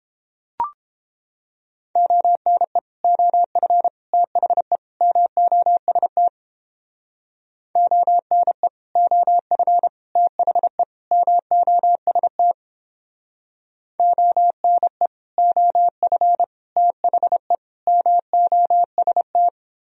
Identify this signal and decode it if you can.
Morse code, heard as a keyed tone, about 29 wpm, 700 Hz. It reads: ONE OF THE MOST ONE OF THE MOST ONE OF THE MOST